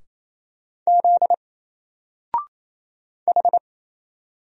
Morse code, heard as a keyed tone, about 27 wpm, 700 Hz. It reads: Z H
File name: Single-Letter-Number-28wpm